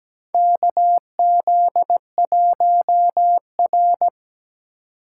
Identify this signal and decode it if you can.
Morse code, heard as a keyed tone, about 17 wpm, 700 Hz. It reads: KZ1R
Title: USA Call Signs - 4 Characters - 3x Character Spacing - 17wpm